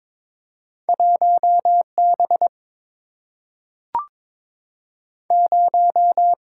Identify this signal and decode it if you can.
Morse code, heard as a keyed tone, about 22 wpm, 700 Hz. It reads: 1B 0
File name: practice / 22wpm / Two Letter-Number Combinations - 22wpm